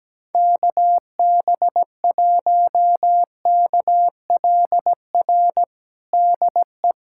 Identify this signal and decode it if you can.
Morse code, heard as a keyed tone, about 17 wpm, 700 Hz. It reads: KB1KLR DE
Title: USA Callsign DE Callsign - 6 Characters - 17wpm